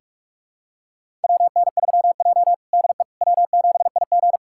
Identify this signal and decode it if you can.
Morse code, heard as a keyed tone, about 46 wpm, 700 Hz. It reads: WN3J DE W7IG
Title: USA Callsign DE Callsign - 4 Characters - 45wpm